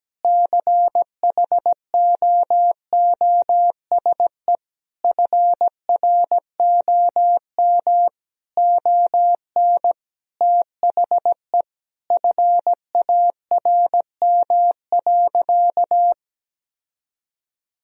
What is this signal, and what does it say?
Morse code, heard as a keyed tone, about 17 wpm, 700 Hz. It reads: CHOOSE FROM ON THE FARM.
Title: Sentences from Top 1000 Words - Review Part 4 - 17wpm